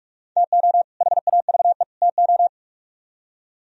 Morse code, heard as a keyed tone, about 57 wpm, 700 Hz. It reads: TO HAVE TO